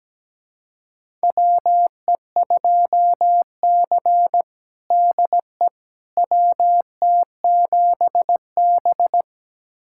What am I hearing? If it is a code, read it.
Morse code, heard as a keyed tone, about 17 wpm, 700 Hz. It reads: WE2C DE WT7B
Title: USA Callsign DE Callsign - 4 Characters - 17wpm